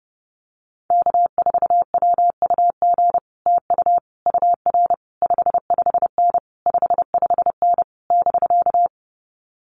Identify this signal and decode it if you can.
Morse code, heard as a keyed tone, about 31 wpm, 700 Hz. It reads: K4WUG TU UR 55N 55N <BK>
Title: Common QSO Phrases Encore 30wpm